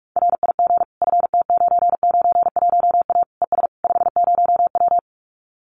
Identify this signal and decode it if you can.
Morse code, heard as a keyed tone, about 44 wpm, 700 Hz. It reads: RIG FT991A ES 50W
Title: Common QSO Phrases Encore 45wpm